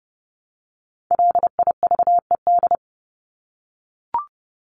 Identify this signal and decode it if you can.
Morse code, heard as a keyed tone, about 30 wpm, 700 Hz. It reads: LIVED
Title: ED Suffix - Top 100 Words with Base Words 30wpm